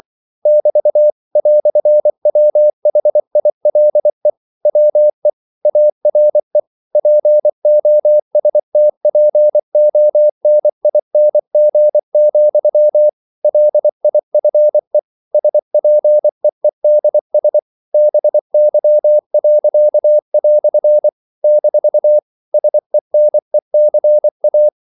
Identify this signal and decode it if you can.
Morse code, heard as a keyed tone, about 24 wpm, 600 Hz. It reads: <BT> "WHILE WE ARE POSTPONING, LIFE SPEEDS BY." - SENECA
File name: CW / Practice / EnglishQuotations0224